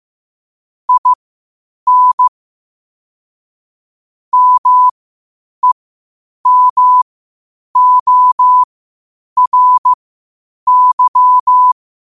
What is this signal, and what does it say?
Morse code, heard as a keyed tone, about 15 wpm, spaced as for 8 wpm, 1000 Hz. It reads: IN MEMORY
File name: qotdfiles / 8wpm